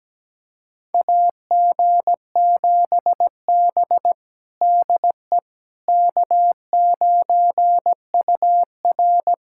Morse code, heard as a keyed tone, about 17 wpm, 700 Hz. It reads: AG7B DE K9UR
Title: USA Callsign DE Callsign - 4 Characters - 17wpm